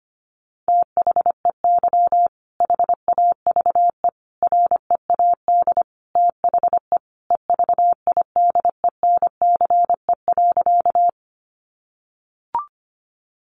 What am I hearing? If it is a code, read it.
Morse code, heard as a keyed tone, about 25 wpm, 700 Hz. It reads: THEY HAVE READ THE EVIDENCE.
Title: Sentences from Top 1300 Words 25wpm